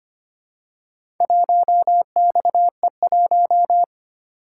Morse code, heard as a keyed tone, about 25 wpm, 700 Hz. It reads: 1XE1
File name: practice / 25wpm / Four Letter-Number Combinations - 25wpm